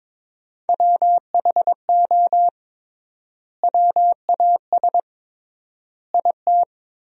Morse code, heard as a keyed tone, about 22 wpm, 700 Hz. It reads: WHO WAS IT
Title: 3x Word-Spacing - Sentences from Top 400 Words - 22wpm